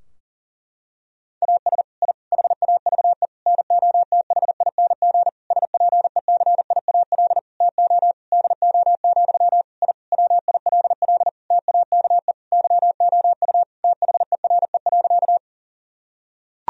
Morse code, heard as a keyed tone, about 37 wpm, 700 Hz. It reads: AS I HAVE NOTHING SPECIAL TO DO, I WILL TAKE YOU THERE.
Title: Sentences from Top 1000 Words - Review Part 3 - 40wpm